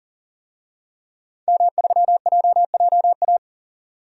Morse code, heard as a keyed tone, about 40 wpm, 700 Hz. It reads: M3JJA
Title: DXCC-100 Call Signs - 5 Characters - 40wpm